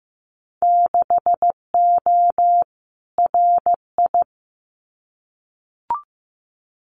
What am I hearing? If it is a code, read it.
Morse code, heard as a keyed tone, about 15 wpm, 700 Hz. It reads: 6O RI
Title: Winter Field Day Exchanges Encore 15wpm